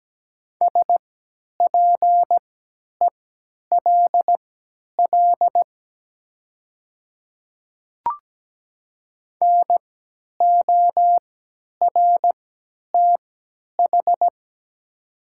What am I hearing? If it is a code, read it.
Morse code, heard as a keyed tone, about 17 wpm, spaced as for 10 wpm, 700 Hz. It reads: SPELL NORTH
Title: 3x Character Spacing - Top 300 Words - 17wpm